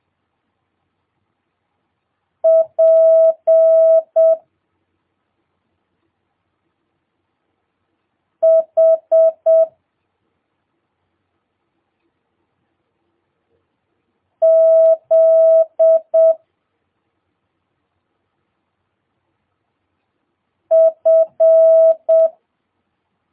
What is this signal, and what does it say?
Morse code, heard as a keyed tone, about 7 wpm, 650 Hz. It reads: P H Z F
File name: jh8cbh / 2022cw